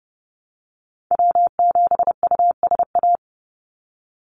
Morse code, heard as a keyed tone, about 30 wpm, 700 Hz. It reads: W7USA
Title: NA SOTA Activator Callsigns 30wpm